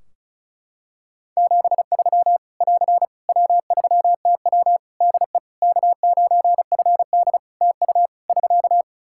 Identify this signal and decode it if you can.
Morse code, heard as a keyed tone, about 36 wpm, 700 Hz. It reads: 73 <AR> W3TW DE K9FD TU <SK>
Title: Practice QSO - Exchange 4 - Ending - 35wpm